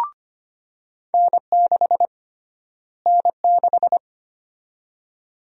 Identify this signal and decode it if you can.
Morse code, heard as a keyed tone, about 25 wpm, 700 Hz. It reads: N6 N6
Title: USA Call Sign Prefixes - Repeated 2x - 25wpm